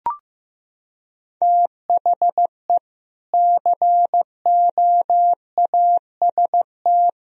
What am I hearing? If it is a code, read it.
Morse code, heard as a keyed tone, about 16 wpm, 700 Hz. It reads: THE COAST
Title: Sentences from Top 1000 Words - Review Part 4 - 15wpm